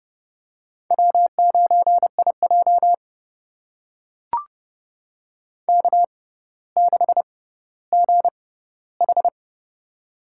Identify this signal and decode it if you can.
Morse code, heard as a keyed tone, about 31 wpm, spaced as for 16 wpm, 700 Hz. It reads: W9IJ K6GH
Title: USA Call Signs - 6x Character Spacing - 30wpm